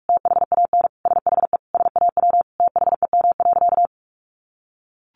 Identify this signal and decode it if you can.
Morse code, heard as a keyed tone, about 44 wpm, 700 Hz. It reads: THAN SHE SAW THEM.
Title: Sentences from Top 1300 Words 45wpm